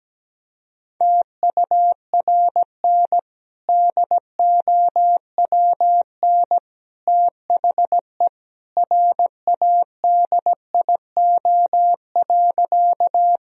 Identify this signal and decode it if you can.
Morse code, heard as a keyed tone, about 17 wpm, 700 Hz. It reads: TURN DOWN THE RADIO.
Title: Sentences from Top 1000 Words - Review Part 3 - 17wpm